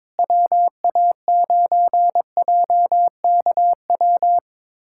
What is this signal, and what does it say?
Morse code, heard as a keyed tone, about 22 wpm, 700 Hz. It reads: WA9JKW
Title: USA Callsign DE Callsign - 6 Characters - 22wpm